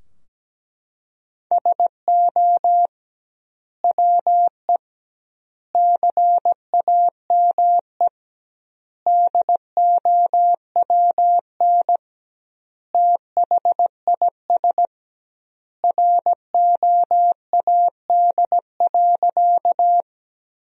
Morse code, heard as a keyed tone, about 17 wpm, 700 Hz. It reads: SO WE CAME DOWN THIS ROAD.
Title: 2x Word-Spacing - Sentences from Top 400 Words - 17wpm